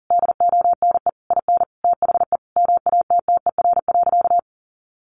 Morse code, heard as a keyed tone, about 40 wpm, 700 Hz. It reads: DONE IN THE MATTER.
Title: Sentences from Top 700 Words 40wpm